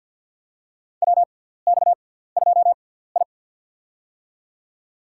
Morse code, heard as a keyed tone, about 49 wpm, spaced as for 17 wpm, 700 Hz. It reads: WX2I
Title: USA Call Signs - 4 Characters - 6x Character Spacing - 50wpm